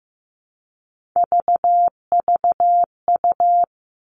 Morse code, heard as a keyed tone, about 15 wpm, 700 Hz. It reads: VVU
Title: Three Challenging Letter-Number Combinations 15wpm